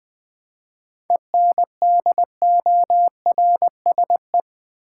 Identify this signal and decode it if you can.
Morse code, heard as a keyed tone, about 20 wpm, 700 Hz. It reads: ENDORSE
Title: 7 Letter Words - First 3 Letters Given - 20wpm